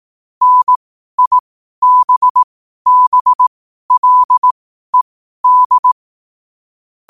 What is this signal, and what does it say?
Morse code, heard as a keyed tone, about 18 wpm, spaced as for 12 wpm, 1000 Hz. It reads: NIBBLED
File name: qotdfiles / 12wpm